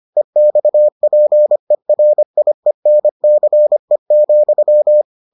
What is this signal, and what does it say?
Morse code, heard as a keyed tone, about 25 wpm, 600 Hz. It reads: EXPERIENCE,